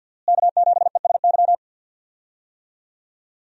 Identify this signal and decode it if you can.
Morse code, heard as a keyed tone, about 47 wpm, 700 Hz. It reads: K7ESY
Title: USA Call Signs - 5 Characters - Repeated 2x - 50wpm